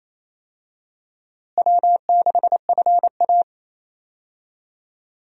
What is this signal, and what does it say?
Morse code, heard as a keyed tone, about 27 wpm, 700 Hz. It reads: W6FA
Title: USA Call Signs - 4 Characters - Repeated 3x - 28wpm